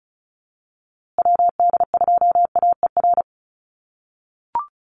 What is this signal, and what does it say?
Morse code, heard as a keyed tone, about 34 wpm, 700 Hz. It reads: WD2AER
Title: USA Call Signs Repeated 3x 35wpm